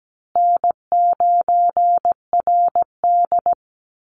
Morse code, heard as a keyed tone, about 17 wpm, 700 Hz. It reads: N9RD
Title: Easy Call Signs Repeated 3x - 17wpm